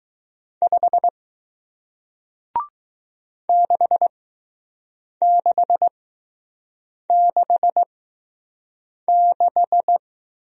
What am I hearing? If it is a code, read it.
Morse code, heard as a keyed tone, about 20 wpm, spaced as for 6 wpm, 700 Hz. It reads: 5 6666